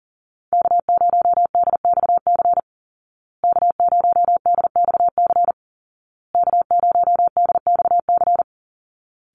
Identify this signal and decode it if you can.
Morse code, heard as a keyed tone, about 40 wpm, 700 Hz. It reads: K0DXC K0DXC K0DXC
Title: Call Signs Repeated 2x - 40wpm